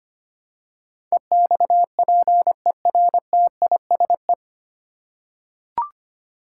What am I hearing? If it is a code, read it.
Morse code, heard as a keyed tone, about 25 wpm, 700 Hz. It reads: EXPERTISE
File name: practice / 25wpm / 9 Letter Words - First 3 Letters Given - 25wpm